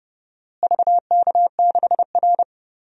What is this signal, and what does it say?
Morse code, heard as a keyed tone, about 32 wpm, 700 Hz. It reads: VK6R
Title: DXCC-100 Call Signs - 4 Characters - Repeated 2x - 30wpm